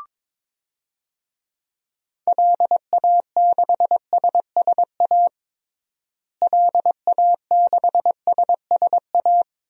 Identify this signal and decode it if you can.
Morse code, heard as a keyed tone, about 21 wpm, 700 Hz. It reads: LA6SSA LA6SSA
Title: DXCC-100 Call Signs - 6 Characters - Repeated 2x - 22wpm